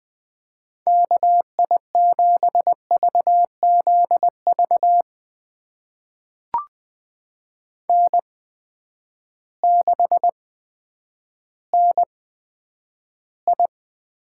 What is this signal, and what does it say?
Morse code, heard as a keyed tone, about 20 wpm, spaced as for 10 wpm, 700 Hz. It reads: KI7VZV N6NI
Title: USA Call Signs - 8x Character Spacing - 20wpm